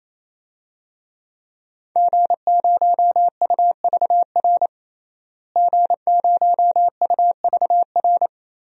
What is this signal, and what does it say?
Morse code, heard as a keyed tone, about 28 wpm, 700 Hz. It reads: G0UVR G0UVR
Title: DXCC-100 Call Signs - 5 Characters - Repeated 3x - 28wpm